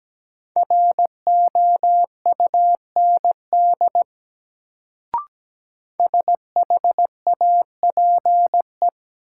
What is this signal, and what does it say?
Morse code, heard as a keyed tone, about 17 wpm, 700 Hz. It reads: ROUND SHAPE
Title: Sets of 5 Words Encore - New Word Drill - 17wpm